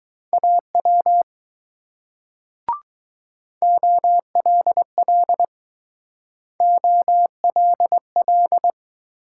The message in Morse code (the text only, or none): AW OLL OLL